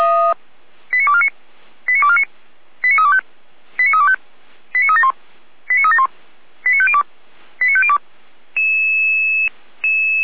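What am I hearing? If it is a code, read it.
Five-tone selective call (ZVEI), heard as quick runs of five tones, tones of about 70 ms. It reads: 89249 89249 89325 89325 89371 89371 89502 89502
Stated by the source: ZVEI five-tone selective call 89250 received on SDR, from feuerwehr-orlamuende fire brigade server.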